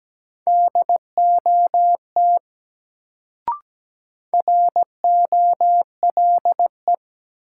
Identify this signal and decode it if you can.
Morse code, heard as a keyed tone, about 17 wpm, 700 Hz. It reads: DOT ROLE
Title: Sentences from Top 1500 Words - New Word Drill Alt - 17wpm